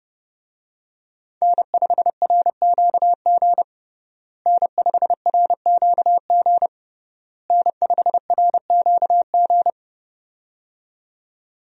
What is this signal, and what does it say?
Morse code, heard as a keyed tone, about 30 wpm, 700 Hz. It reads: N5RQG N5RQG N5RQG
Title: USA Call Signs - 5 Characters - Repeated 3x - 30wpm